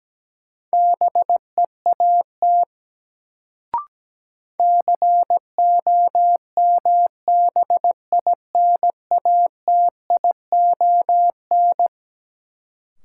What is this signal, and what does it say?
Morse code, heard as a keyed tone, about 17 wpm, 700 Hz. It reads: BEAT COMBINATION
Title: Sentences from Top 1300 Words - New Word Drill Alt - 17wpm